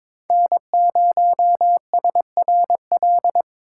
Morse code, heard as a keyed tone, about 22 wpm, 700 Hz. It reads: N0SRL
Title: USA Call Signs - 5 Characters - 3x Character Spacing - 22wpm